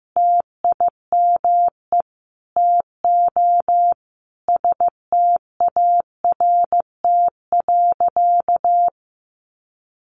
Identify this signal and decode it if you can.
Morse code, heard as a keyed tone, about 14 wpm, 700 Hz. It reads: TIME TO START.
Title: Sentences from Top 300 Words 15wpm